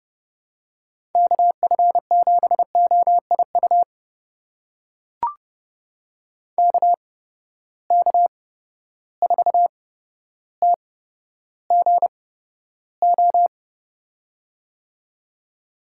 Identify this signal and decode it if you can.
Morse code, heard as a keyed tone, about 30 wpm, spaced as for 13 wpm, 700 Hz. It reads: KF7OIU KK4TGO
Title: USA Call Signs - 6 Characters - 8x Character Spacing - 30wpm